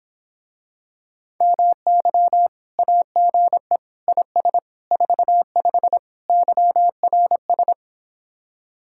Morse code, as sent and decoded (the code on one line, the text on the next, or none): -- -.-- / .- --. / .. ... / ....- ..... / -.-- .-. ...
MY AGE IS 45 YRS